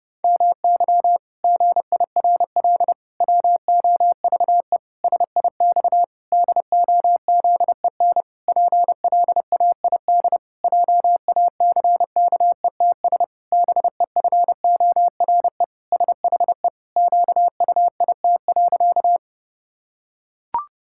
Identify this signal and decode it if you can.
Morse code, heard as a keyed tone, about 32 wpm, 700 Hz. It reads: MY GIRL WOVE SIX DOZEN PLAID JACKETS BEFORE SHE QUIT.
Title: Pangram Challenge v2 - 30wpm